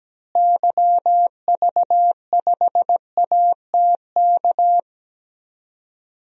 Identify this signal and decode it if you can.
Morse code, heard as a keyed tone, about 17 wpm, 700 Hz. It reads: YV5ATK
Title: DXCC-100 Call Signs - 6 Characters - Repeated 3x - 17wpm